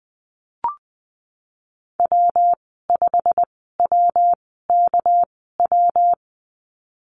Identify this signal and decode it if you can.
Morse code, heard as a keyed tone, about 20 wpm, spaced as for 14 wpm, 700 Hz. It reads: W5WKW
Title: 245 Lesson 39g - Letter z - Call Signs 20wpm